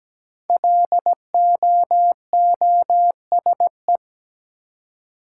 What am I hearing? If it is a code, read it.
Morse code, heard as a keyed tone, about 17 wpm, 700 Hz. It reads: LOOSE
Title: Five Letter Words - 17wpm